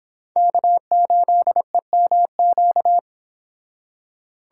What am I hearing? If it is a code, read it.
Morse code, heard as a keyed tone, about 25 wpm, 700 Hz. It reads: K8EMQ